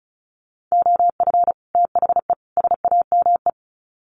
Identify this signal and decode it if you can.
Morse code, heard as a keyed tone, about 34 wpm, 700 Hz. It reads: OF THE SAME
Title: Sets of 3 Words 35wpm